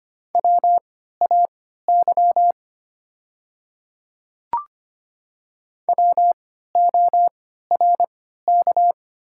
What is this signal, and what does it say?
Morse code, heard as a keyed tone, about 25 wpm, spaced as for 14 wpm, 700 Hz. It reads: WAY WORK
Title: 3x Character Spacing - Top 200 Words - 25wpm